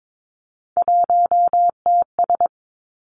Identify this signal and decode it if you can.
Morse code, heard as a keyed tone, about 22 wpm, 700 Hz. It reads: 1TS
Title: Three Challenging Letter-Number Combinations 22wpm